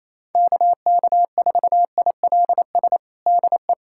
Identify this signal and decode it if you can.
Morse code, heard as a keyed tone, about 27 wpm, 700 Hz. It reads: KK4ILS DE
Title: USA Callsign DE Callsign - 6 Characters - 28wpm